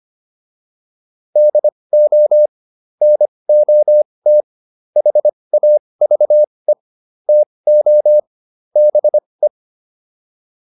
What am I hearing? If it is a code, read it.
Morse code, heard as a keyed tone, about 25 wpm, spaced as for 20 wpm, 600 Hz. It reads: DO NOT HAVE TO BE
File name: WD405 20